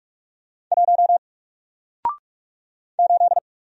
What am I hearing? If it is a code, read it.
Morse code, heard as a keyed tone, about 46 wpm, 700 Hz. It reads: 1 8
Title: Single Character (Letter:Number:Punctuation:Prosign) Alt 45wpm